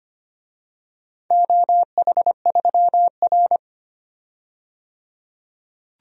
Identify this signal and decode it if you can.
Morse code, heard as a keyed tone, about 25 wpm, 700 Hz. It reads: OH3R
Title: DXCC-100 Call Signs - 4 Characters - 25wpm